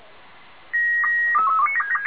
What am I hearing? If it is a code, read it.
SSTV (slow-scan television), Robot 36.
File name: lu4aao / Betty IV x LU1EZN